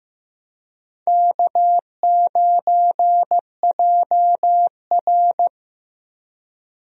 Morse code, heard as a keyed tone, about 15 wpm, 700 Hz. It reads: K9JR
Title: USA Call Signs - 4 Characters - 15wpm